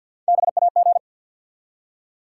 Morse code, heard as a keyed tone, about 54 wpm, 700 Hz. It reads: BAG
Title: Three Letter Words - 50wpm